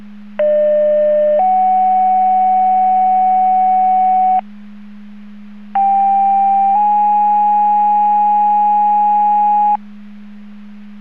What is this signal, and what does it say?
Two-tone page, heard as a one-second tone, then a three-second tone, about 600 then 750 Hz, about 790 then 830 Hz. It is heard from 0.4 to 9.8 s.
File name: Portage 154.385 Paging